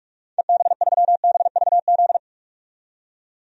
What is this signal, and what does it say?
Morse code, heard as a keyed tone, about 46 wpm, 700 Hz. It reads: EB3BVZ
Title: DXCC-100 Call Signs - 6 Characters - 45wpm